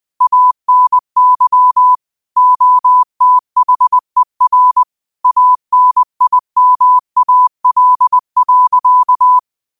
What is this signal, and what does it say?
Morse code, heard as a keyed tone, about 20 wpm, 1000 Hz. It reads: ANY OTHER ANIMAL.
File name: qotdfiles / 20wpm